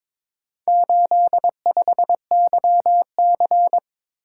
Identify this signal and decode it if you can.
Morse code, heard as a keyed tone, about 23 wpm, 700 Hz. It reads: 85YC
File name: practice / 22wpm / Four Letter-Number Combinations - 22wpm